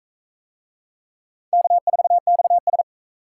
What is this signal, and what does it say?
Morse code, heard as a keyed tone, about 43 wpm, 700 Hz. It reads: K4XS